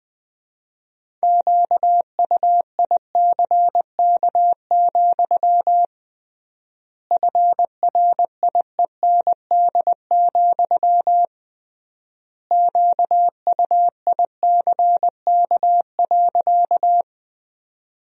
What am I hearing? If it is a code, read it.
Morse code, heard as a keyed tone, about 20 wpm, 700 Hz. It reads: QUICK, FRIEND, QUICK.
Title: 3x Word-Spacing - Sentences from Top 500 Words - 20wpm